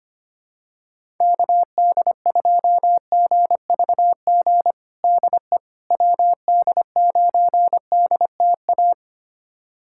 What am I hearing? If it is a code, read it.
Morse code, heard as a keyed tone, about 25 wpm, 700 Hz. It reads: KD2GVG DE WD9DTA
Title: USA Callsign DE Callsign - 6 Characters - 25wpm